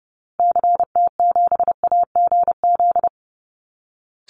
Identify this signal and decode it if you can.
Morse code, heard as a keyed tone, about 30 wpm, 700 Hz. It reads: CT7AGZ
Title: Call Signs Repeated 2x - 30wpm